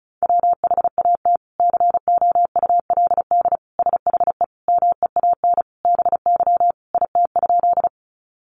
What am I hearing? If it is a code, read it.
Morse code, heard as a keyed tone, about 34 wpm, 700 Hz. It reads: WHAT COULD SHE MEAN BY IT?
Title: Sentences from Top 200 Words 35wpm